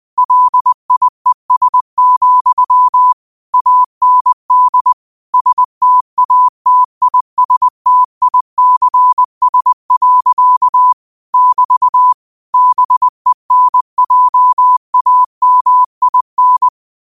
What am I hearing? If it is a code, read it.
Morse code, heard as a keyed tone, about 20 wpm, 1000 Hz. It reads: LIES, AND STATISTICS. <BT> BENJAMIN